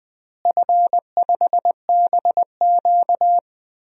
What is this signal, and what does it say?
Morse code, heard as a keyed tone, about 20 wpm, 700 Hz. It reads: F5BQ
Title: DXCC-100 Call Signs - 4 Characters - Repeated 2x - 20wpm